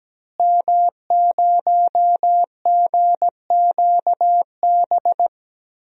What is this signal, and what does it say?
Morse code, heard as a keyed tone, about 17 wpm, 700 Hz. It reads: M0GQB
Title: DXCC-100 Call Signs - 5 Characters - Repeated 3x - 17wpm